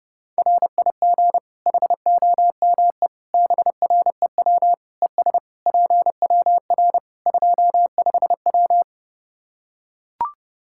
Morse code, heard as a keyed tone, about 32 wpm, 700 Hz. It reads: RIG HOMEBREW ES PWR 25W
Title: Practice QSO - Exchange 2 - Info - 30wpm